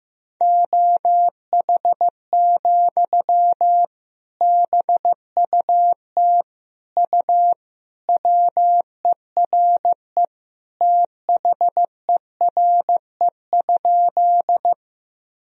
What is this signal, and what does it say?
Morse code, heard as a keyed tone, about 15 wpm, 700 Hz. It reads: OH, BUT U WERE THERE?
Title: Sentences from Top 500 Words and Abbreviations 15wpm